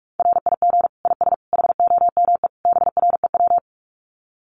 Morse code, heard as a keyed tone, about 44 wpm, 700 Hz. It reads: RIG IS HOME BREW
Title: Common QSO Phrases 45wpm